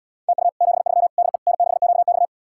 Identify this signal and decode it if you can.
Morse code, heard as a keyed tone, about 78 wpm, 700 Hz. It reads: TU 73 DE N6QX